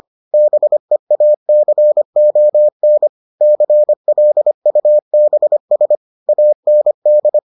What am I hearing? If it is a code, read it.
Morse code, heard as a keyed tone, about 25 wpm, 600 Hz. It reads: BEACON CLUBS AND